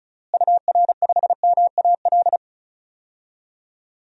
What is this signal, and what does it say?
Morse code, heard as a keyed tone, about 34 wpm, 700 Hz. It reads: UR5MAL
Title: DXCC-100 Call Signs - 6 Characters - Repeated 3x - 35wpm